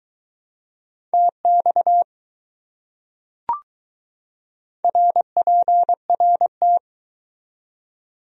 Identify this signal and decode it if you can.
Morse code, heard as a keyed tone, about 20 wpm, 700 Hz. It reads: TX RPRT